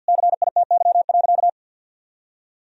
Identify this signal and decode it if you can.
Morse code, heard as a keyed tone, about 51 wpm, 700 Hz. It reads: CITY.
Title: Sentences from Top 1000 Words - Review Part 2 - 50wpm